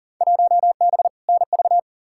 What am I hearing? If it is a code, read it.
Morse code, heard as a keyed tone, about 40 wpm, 700 Hz. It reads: 1B NV
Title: Field Day Exchanges - 40wpm